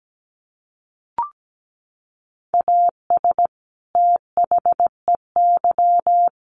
Morse code, heard as a keyed tone, about 17 wpm, 700 Hz. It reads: AS THEY
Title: Sets of 2 Words 17wpm